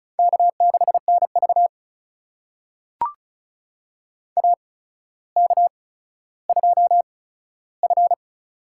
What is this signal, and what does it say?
Morse code, heard as a keyed tone, about 36 wpm, spaced as for 15 wpm, 700 Hz. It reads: K6NV AK2F
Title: USA Call Signs - 4 Characters - 8x Character Spacing - 35wpm